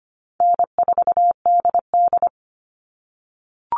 Morse code, heard as a keyed tone, about 25 wpm, 700 Hz. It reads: N4DD
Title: All Cut Numbers - 4-Digit 25wpm